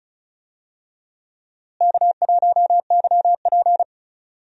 Morse code, heard as a keyed tone, about 34 wpm, 700 Hz. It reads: K1YP
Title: USA Call Signs - 4 Characters - Repeated 3x - 35wpm